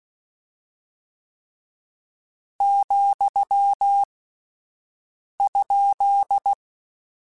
Morse code, heard as a keyed tone, about 16 wpm, 800 Hz. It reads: , ?
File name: Misc / Misc-mp3s / Punct @10